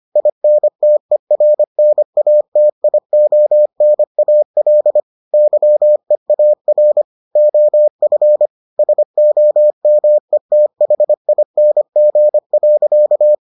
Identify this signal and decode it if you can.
Morse code, heard as a keyed tone, about 25 wpm, 600 Hz. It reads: INTERNATIONAL YEAR OF SOMETHING.